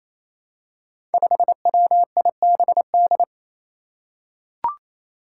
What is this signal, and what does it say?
Morse code, heard as a keyed tone, about 27 wpm, 700 Hz. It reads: 5WIBD
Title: Five Challenging Letter-Number Combinations 28wpm